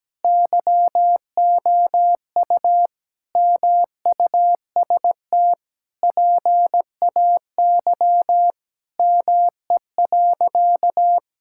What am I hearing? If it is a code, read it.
Morse code, heard as a keyed tone, about 17 wpm, 700 Hz. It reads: YOU MUST PAY ME.
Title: Sentences from Top 1000 Words - Review Part 5 - 17wpm